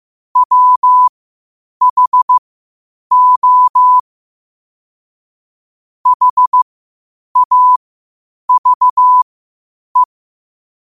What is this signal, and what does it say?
Morse code, heard as a keyed tone, about 15 wpm, spaced as for 8 wpm, 1000 Hz. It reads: WHO HAVE
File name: qotdfiles / 8wpm